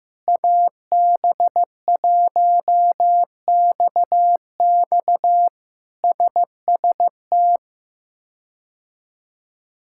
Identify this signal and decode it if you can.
Morse code, heard as a keyed tone, about 15 wpm, 700 Hz. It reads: AB1XX SST
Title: K1USN SST QSO Practice - 15wpm